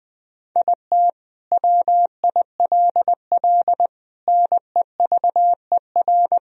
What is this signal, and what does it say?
Morse code, heard as a keyed tone, about 19 wpm, 700 Hz. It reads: IT WILL NEVER
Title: Sentences from Top 1000 Words - Review Part 2 - 20wpm